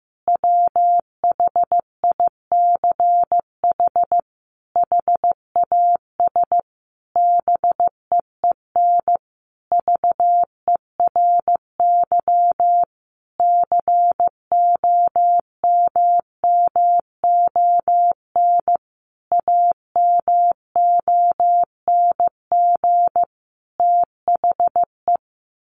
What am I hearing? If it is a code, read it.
Morse code, heard as a keyed tone, about 15 wpm, 700 Hz. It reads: WHICH HAS BEEN VERY COMMON AMONG THE
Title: Sentences from Top 1400 Words 15wpm